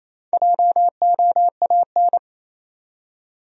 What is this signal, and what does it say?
Morse code, heard as a keyed tone, about 28 wpm, 700 Hz. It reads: JOAN
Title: USA Common Names - Max 5 Letters - 28wpm